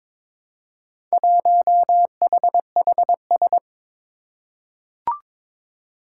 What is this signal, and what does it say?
Morse code, heard as a keyed tone, about 22 wpm, 700 Hz. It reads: 1HHS
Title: Four Challenging Letter-Number Combinations 22wpm